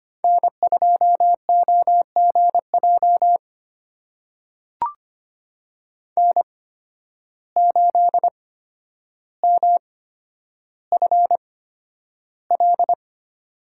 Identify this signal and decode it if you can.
Morse code, heard as a keyed tone, about 25 wpm, spaced as for 11 wpm, 700 Hz. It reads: N2OGJ N8MFL
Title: USA Call Signs - 5 Characters - 8x Character Spacing - 25wpm